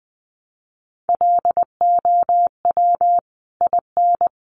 Morse code, heard as a keyed tone, about 20 wpm, 700 Hz. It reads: LOW IN
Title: Sentences from Top 900 Words 20wpm